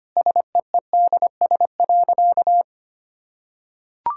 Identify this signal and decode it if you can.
Morse code, heard as a keyed tone, about 25 wpm, 700 Hz. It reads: SEEDS.
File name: Easy Sentences with Hints - 25wpm